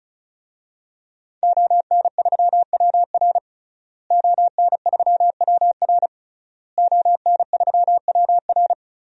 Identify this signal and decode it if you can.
Morse code, heard as a keyed tone, about 34 wpm, 690 Hz. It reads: ON3WR ON3WR ON3WR
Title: DXCC-100 Call Signs - 5 Characters - Repeated 3x - 35wpm